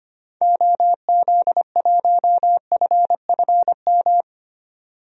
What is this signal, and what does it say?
Morse code, heard as a keyed tone, about 25 wpm, 700 Hz. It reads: OZ1FFM
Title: DXCC-100 Call Signs - 6 Characters - 25wpm